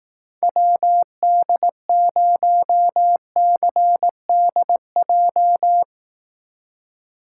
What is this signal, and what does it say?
Morse code, heard as a keyed tone, about 17 wpm, 700 Hz. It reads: WD0CDJ
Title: Speed Racing - USA Call Signs 15wpm